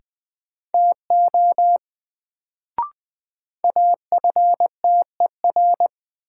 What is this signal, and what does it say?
Morse code, heard as a keyed tone, about 20 wpm, 700 Hz. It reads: TO AFTER
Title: Sets of 2 Words - New Word Drill - 20wpm